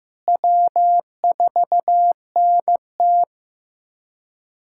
Morse code, heard as a keyed tone, about 15 wpm, 700 Hz. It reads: W4NT
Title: USA Callsign DE Callsign - 4 Characters - 15wpm